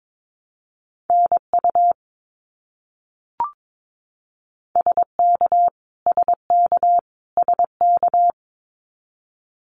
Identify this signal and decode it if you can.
Morse code, heard as a keyed tone, about 22 wpm, 700 Hz. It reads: NU SK SK SK
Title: Canadian Provinces and Territories - Repeated 3x - 22wpm